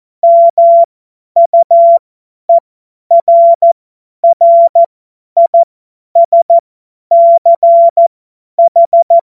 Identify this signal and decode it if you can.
Morse code, heard as a keyed tone, about 14 wpm, spaced as for 10 wpm, 680 Hz. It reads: MUERRISCH
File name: Morsekurs / Witze / Witz 23 Januar 2024 50 F70